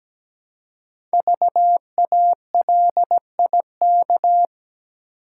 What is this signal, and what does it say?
Morse code, heard as a keyed tone, about 17 wpm, 700 Hz. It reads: VALIK
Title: Code Groups - 5 Letters - 17wpm